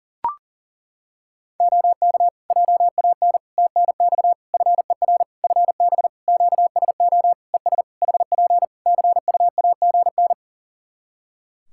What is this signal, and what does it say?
Morse code, heard as a keyed tone, about 40 wpm, 700 Hz. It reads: OK JAN TNX FER FB QSO ES HP CUAGN
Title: K1USN SST QSO Practice 40wpm